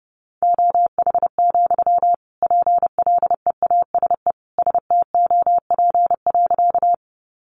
Morse code, heard as a keyed tone, about 30 wpm, 700 Hz. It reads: OH, PLEASE STOP.
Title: Sentences from Top 1500 Words 30wpm